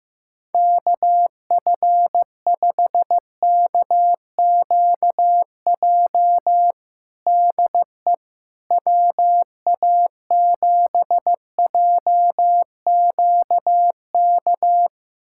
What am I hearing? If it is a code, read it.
Morse code, heard as a keyed tone, about 15 wpm, 700 Hz. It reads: KF5KQJ DE WA7JQK
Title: USA Callsign DE Callsign 15wpm